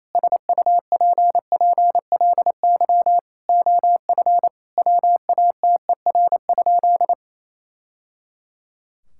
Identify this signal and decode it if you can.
Morse code, heard as a keyed tone, about 28 wpm, 700 Hz. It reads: SUPPLY OF WATER?
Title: Sentences from Top 1000 Words - Review Part 5 - 28wpm